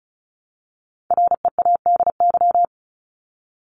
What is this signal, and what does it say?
Morse code, heard as a keyed tone, about 34 wpm, 700 Hz. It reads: READY
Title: Top 500 Words 35wpm